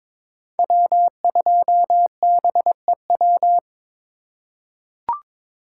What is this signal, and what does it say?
Morse code, heard as a keyed tone, about 21 wpm, 700 Hz. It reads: W2BEW
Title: USA Call Signs - 5 Characters - 4x Character Spacing - 22wpm